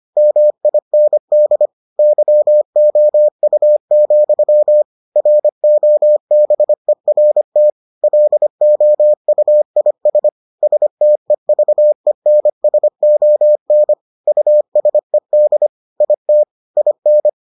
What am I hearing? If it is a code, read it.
Morse code, heard as a keyed tone, about 25 wpm, 600 Hz. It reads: MIND YOU, ROBERT LOUIS STEVENSON USED IT IN